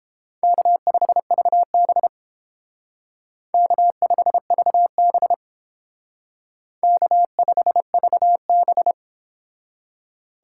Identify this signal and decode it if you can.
Morse code, heard as a keyed tone, about 30 wpm, 700 Hz. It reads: K5VB K5VB K5VB